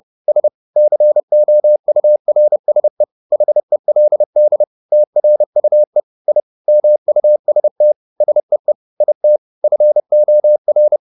Morse code, heard as a keyed tone, about 30 wpm, 600 Hz. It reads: S COURSE HELD TRUE I MUST SEE IT FOR
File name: the old man and the sea / the old man and the sea 1 30wpm